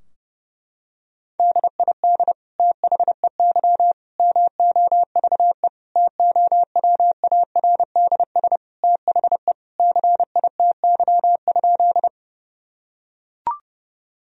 Morse code, heard as a keyed tone, about 30 wpm, 700 Hz. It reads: DID THEY MOVE TOWARDS THE CITY?
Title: Sentences from Top 1000 Words - Review Part 2 - 30wpm